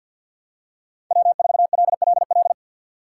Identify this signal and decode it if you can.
Morse code, heard as a keyed tone, about 53 wpm, 700 Hz. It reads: W4LLL
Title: USA Call Signs - 5 Characters - 8x Character Spacing - 50wpm